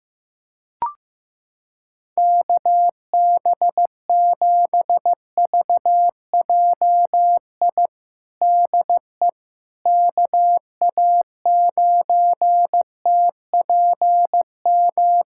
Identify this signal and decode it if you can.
Morse code, heard as a keyed tone, about 15 wpm, 700 Hz. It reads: KB7VJI DE KA9TPM
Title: USA Callsign DE Callsign - 6 Characters - 15wpm